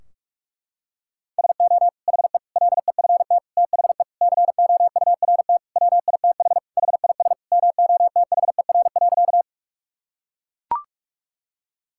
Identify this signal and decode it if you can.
Morse code, heard as a keyed tone, about 41 wpm, 700 Hz. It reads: SO HE LEFT THE COURT WITH HIS MOTHER.
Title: Sentences from Top 1600 Words 45wpm